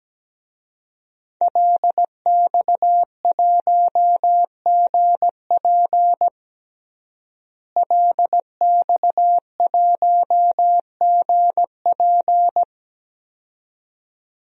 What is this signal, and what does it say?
Morse code, heard as a keyed tone, about 17 wpm, 700 Hz. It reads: LX1GP LX1GP
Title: DXCC-100 Call Signs - 5 Characters - Repeated 2x - 17wpm